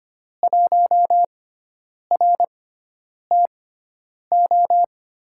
Morse code, heard as a keyed tone, about 25 wpm, spaced as for 9 wpm, 700 Hz. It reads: 1RTO
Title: USA Call Signs - 6x Character Spacing - 25wpm